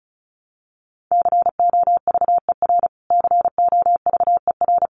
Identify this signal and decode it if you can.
Morse code, heard as a keyed tone, about 34 wpm, 700 Hz. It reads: COVER COVER
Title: Top 500 Words Repeated 35wpm